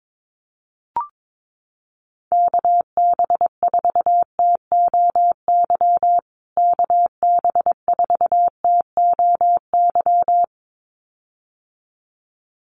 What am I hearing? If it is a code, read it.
Morse code, heard as a keyed tone, about 22 wpm, 700 Hz. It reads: KB4TOY KB4TOY
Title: USA Call Signs Repeated 2x 22wpm